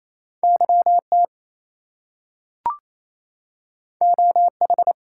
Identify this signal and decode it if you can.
Morse code, heard as a keyed tone, about 28 wpm, 700 Hz. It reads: YT OH
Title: US States and Canadian Provinces 28wpm